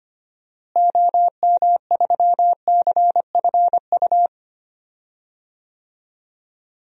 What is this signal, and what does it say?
Morse code, heard as a keyed tone, about 25 wpm, 700 Hz. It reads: OM3CFU
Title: DXCC-100 Call Signs - 6 Characters - Repeated 3x - 25wpm